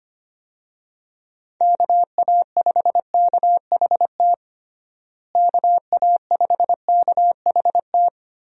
Morse code, heard as a keyed tone, about 25 wpm, 700 Hz. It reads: KA5KHT KA5KHT
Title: USA Call Signs - 6 Characters - Repeated 3x - 25wpm